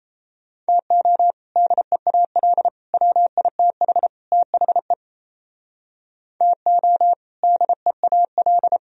Morse code, heard as a keyed tone, about 34 wpm, 700 Hz. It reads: TO DEAL WITH THE TO DEAL
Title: Speed-Racing - Sets of 4 Words 28wpm